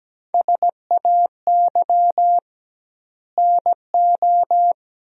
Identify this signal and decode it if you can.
Morse code, heard as a keyed tone, about 17 wpm, 700 Hz. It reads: SAY NO
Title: 2x Word-Spacing - Sentences from Top 200 Words - 17wpm